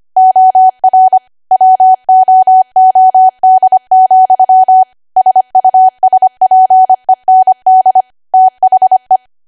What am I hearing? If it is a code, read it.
Morse code, heard as a keyed tone, about 25 wpm, 750 Hz. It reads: OR WOOD, SUSPEND THE